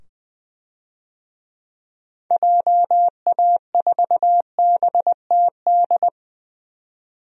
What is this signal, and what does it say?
Morse code, heard as a keyed tone, about 20 wpm, 700 Hz. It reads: JA4BTD
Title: DXCC-100 Call Signs - 6 Characters - 20wpm